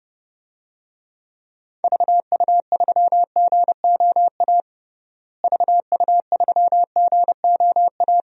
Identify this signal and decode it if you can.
Morse code, heard as a keyed tone, about 30 wpm, 700 Hz. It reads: VU3GOA VU3GOA
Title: DXCC-100 Call Signs - 6 Characters - Repeated 2x - 30wpm